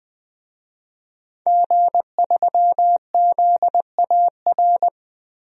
Morse code, heard as a keyed tone, about 21 wpm, 700 Hz. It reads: G3ZAR
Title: DXCC-100 Call Signs - 5 Characters - 20wpm